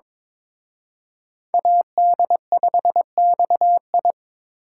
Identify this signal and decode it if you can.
Morse code, heard as a keyed tone, about 26 wpm, 700 Hz. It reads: AD5XI
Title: Speed Racing - USA 2x2 Call Signs 22wpm